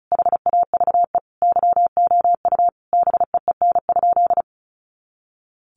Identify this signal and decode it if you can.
Morse code, heard as a keyed tone, about 34 wpm, 700 Hz. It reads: HAVE YOU BEEN?
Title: Sentences from Top 400 Words 35wpm